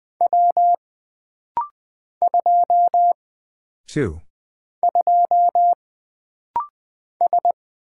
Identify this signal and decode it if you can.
Morse code, heard as a keyed tone, about 20 wpm, 700 Hz. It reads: W 2 2 S
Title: Single Character (Letter:Number:Punctuation:Prosign) - Rapid-Fire - 20wpm